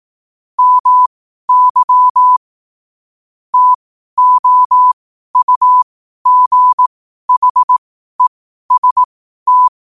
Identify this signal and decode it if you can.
Morse code, heard as a keyed tone, about 18 wpm, spaced as for 12 wpm, 1000 Hz. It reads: MY TOUGHEST
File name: qotdfiles / 12wpm